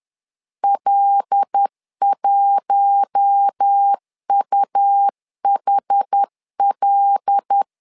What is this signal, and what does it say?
Morse code, heard as a keyed tone, about 10 wpm, 800 Hz. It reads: L1UHL
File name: LEZIONI / 5 LBVC / 5 L 10wpm 2259